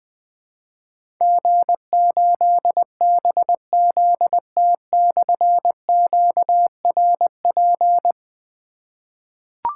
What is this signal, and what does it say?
Morse code, heard as a keyed tone, about 21 wpm, 700 Hz. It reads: G8BZT/QRP
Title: Difficult Call Signs 20wpm